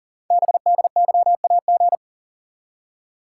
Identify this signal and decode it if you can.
Morse code, heard as a keyed tone, about 44 wpm, 700 Hz. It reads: BDYAG
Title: Code Groups - 5 Letters - 40wpm